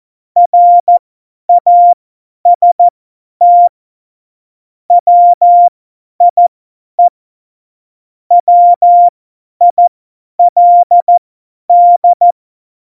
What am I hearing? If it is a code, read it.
Morse code, heard as a keyed tone, about 14 wpm, spaced as for 10 wpm, 700 Hz. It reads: RAST WIE WILD